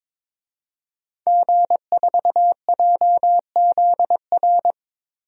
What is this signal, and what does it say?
Morse code, heard as a keyed tone, about 22 wpm, 700 Hz. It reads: G4JZR
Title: DXCC-100 Call Signs - 5 Characters - Repeated 3x - 22wpm